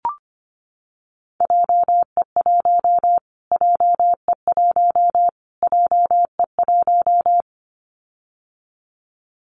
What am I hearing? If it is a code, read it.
Morse code, heard as a keyed tone, about 25 wpm, 700 Hz. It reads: JE1 JE1 JE1
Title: Callsign Prefixes for Top 25 DXCC - Repeated 3x - 25wpm